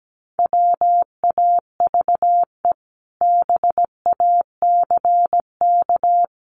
Morse code, heard as a keyed tone, about 17 wpm, 700 Hz. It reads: WAVE BACK
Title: Sentences from Top 1400 Words 17wpm